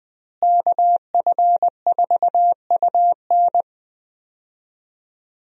Morse code, heard as a keyed tone, about 21 wpm, 700 Hz. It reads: KF4UN
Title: USA Call Signs - 5 Characters - Repeated 3x - 20wpm